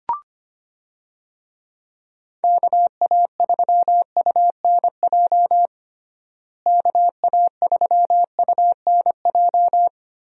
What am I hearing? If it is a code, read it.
Morse code, heard as a keyed tone, about 25 wpm, 700 Hz. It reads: KA3UNJ KA3UNJ
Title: USA Call Signs - 6 Characters - Repeated 3x - 25wpm